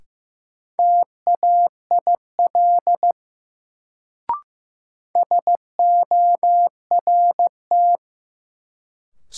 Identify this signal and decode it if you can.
Morse code, heard as a keyed tone, about 15 wpm, 700 Hz. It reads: TAIL SORT
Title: Sentences from Top 1000 Words - Review Part 3 - New Word Drill - 15wpm